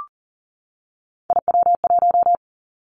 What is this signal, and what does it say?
Morse code, heard as a keyed tone, about 39 wpm, 700 Hz. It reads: IW1
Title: Callsign Prefixes for Top 50 DXCC - 40wpm